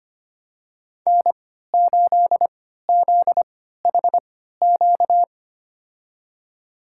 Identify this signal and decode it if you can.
Morse code, heard as a keyed tone, about 25 wpm, spaced as for 14 wpm, 700 Hz. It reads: N8ZHQ
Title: USA Call Signs - 3x Character Spacing - 25wpm